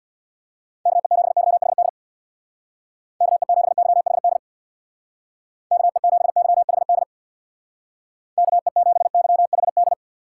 Morse code, heard as a keyed tone, about 62 wpm, 700 Hz. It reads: KE7YHD KE7YHD KE7YHD KE7YHD
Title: Speed Racing - USA 2x3 Call Signs 50wpm